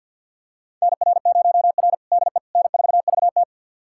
Morse code, heard as a keyed tone, about 50 wpm, 700 Hz. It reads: NR0R DE N4VT
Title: USA Callsign DE Callsign - 4 Characters - 50wpm